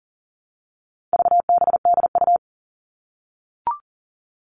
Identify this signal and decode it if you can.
Morse code, heard as a keyed tone, about 40 wpm, 700 Hz. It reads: VBDU
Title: All Cut Numbers - 4-Digit 40wpm